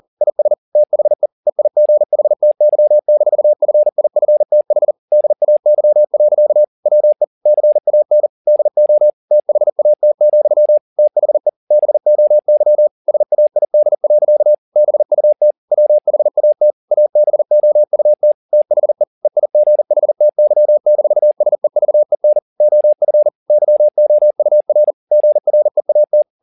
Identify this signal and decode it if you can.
Morse code, heard as a keyed tone, about 40 wpm, 600 Hz. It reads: IS THE EIGHTY-FIFTH DAY. WE CAN DO THAT, THE BOY SAID. BUT WHAT ABOUT THE EIGHTY-SEVEN OF YOUR GREAT